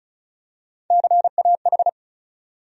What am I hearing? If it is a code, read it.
Morse code, heard as a keyed tone, about 36 wpm, 700 Hz. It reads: CAH
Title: USA Call Sign Postfixes - 3 Letters 35wpm